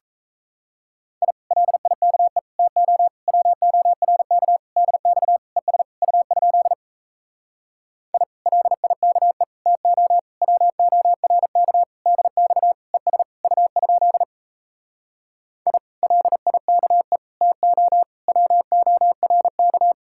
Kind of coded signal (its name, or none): Morse code